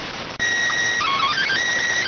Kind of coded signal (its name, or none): SSTV (slow-scan television)